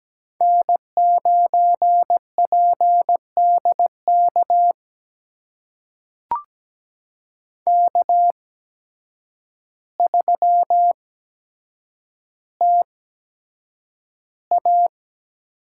Morse code, heard as a keyed tone, about 17 wpm, spaced as for 8 wpm, 700 Hz. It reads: N9PDK K3TA
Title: USA Call Signs - 5 Characters - 8x Character Spacing - 17wpm